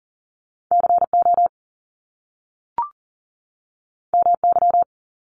Morse code, heard as a keyed tone, about 40 wpm, 700 Hz. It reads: CO MY
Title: Top Two Letter Combinations in Words 40wpm